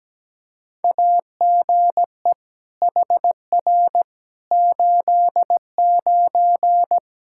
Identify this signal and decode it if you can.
Morse code, heard as a keyed tone, about 17 wpm, 700 Hz. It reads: AGE HR 89
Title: Practice QSO - Exchange 3 - More Info - 17wpm